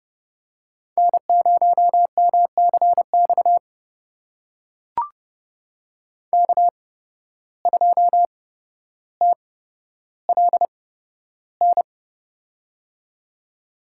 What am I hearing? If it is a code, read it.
Morse code, heard as a keyed tone, about 31 wpm, spaced as for 13 wpm, 700 Hz. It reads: N0MCX K2TLN
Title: USA Call Signs - 5 Characters - 8x Character Spacing - 30wpm